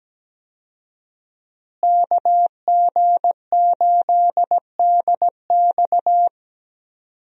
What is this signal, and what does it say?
Morse code, heard as a keyed tone, about 17 wpm, 700 Hz. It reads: KG8DX
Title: USA Call Signs - 5 Characters - Repeated 2x - 17wpm